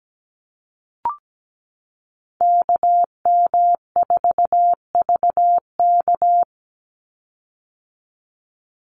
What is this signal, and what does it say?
Morse code, heard as a keyed tone, about 17 wpm, 700 Hz. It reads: KM4VK
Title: USA Call Signs 17wpm